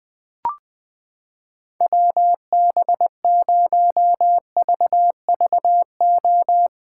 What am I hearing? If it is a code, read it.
Morse code, heard as a keyed tone, about 20 wpm, 700 Hz. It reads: WB0VVO
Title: USA Callsign DE Callsign - 6 Characters - 20wpm